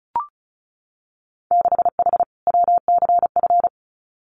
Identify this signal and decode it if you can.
Morse code, heard as a keyed tone, about 36 wpm, 700 Hz. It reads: 6H WCF
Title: Winter Field Day Exchanges Encore 35wpm